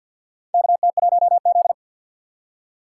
Morse code, heard as a keyed tone, about 52 wpm, 700 Hz. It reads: KT1Z
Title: USA Call Signs - 4 Characters - 6x Character Spacing - 50wpm